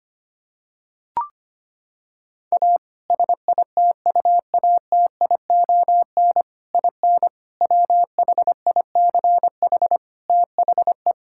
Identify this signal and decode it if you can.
Morse code, heard as a keyed tone, about 25 wpm, 700 Hz. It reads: A SITUATION IN WHICH THE
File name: practice / 25wpm / Sentences with 3-Word Phrases - 25wpm